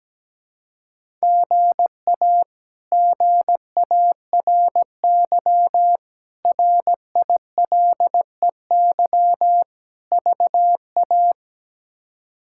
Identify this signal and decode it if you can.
Morse code, heard as a keyed tone, about 17 wpm, 700 Hz. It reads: GA GARY RILEY VA
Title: K1USN SST QSO Practice - 17wpm